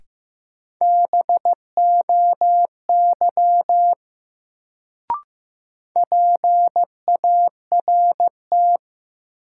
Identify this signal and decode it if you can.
Morse code, heard as a keyed tone, about 15 wpm, 700 Hz. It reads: BOY PART
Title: Sentences from Top 200 Words - New Word Drill - 15wpm